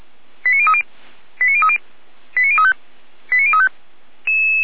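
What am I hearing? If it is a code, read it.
Five-tone selective call (ZVEI), heard as quick runs of five tones, tones of about 70 ms. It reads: 89030 89030 89035 89035
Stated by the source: ZVEI five-tone call 89050 received by SDR.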